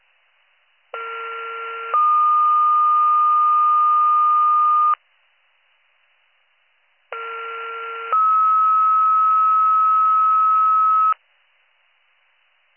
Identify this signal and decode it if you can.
Two-tone page, heard as a one-second tone, then a three-second tone, about 1370 then 1220 Hz, about 1370 then 1290 Hz. It is heard from 0.9 to 11.1 s.